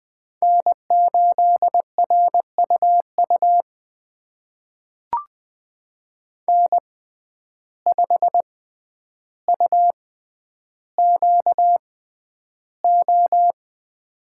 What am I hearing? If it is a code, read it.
Morse code, heard as a keyed tone, about 20 wpm, spaced as for 10 wpm, 700 Hz. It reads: N8RUU N5UQO
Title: USA Call Signs - 5 Characters - 6x Character Spacing - 20wpm